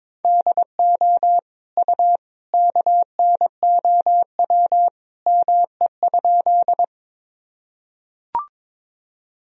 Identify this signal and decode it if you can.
Morse code, heard as a keyed tone, about 22 wpm, 700 Hz. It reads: DO U KNOW ME?
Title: Sentences from Top 200 Words and Abbreviations 22wpm